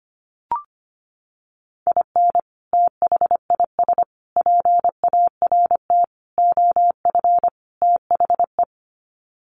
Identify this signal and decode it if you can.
Morse code, heard as a keyed tone, about 25 wpm, 700 Hz. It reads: IN THIS PART OF THE
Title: Sets of 5 Words 25wpm